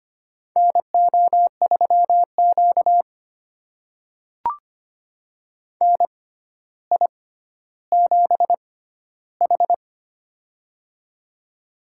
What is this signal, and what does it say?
Morse code, heard as a keyed tone, about 25 wpm, spaced as for 13 wpm, 700 Hz. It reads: NO3Q NI7H
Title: USA Call Signs - 4 Characters - 6x Character Spacing - 25wpm